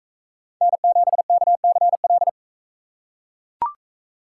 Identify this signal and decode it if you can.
Morse code, heard as a keyed tone, about 39 wpm, 700 Hz. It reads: N7KCL